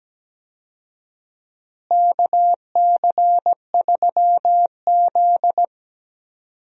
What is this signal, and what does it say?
Morse code, heard as a keyed tone, about 17 wpm, 700 Hz. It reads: KC3Z
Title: USA Call Signs - 4 Characters - Repeated 3x - 17wpm